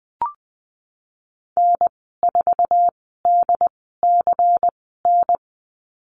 Morse code, heard as a keyed tone, about 19 wpm, spaced as for 14 wpm, 700 Hz. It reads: N4DCN
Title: 122 Lesson 20g - Number 3 - Call Signs 20wpm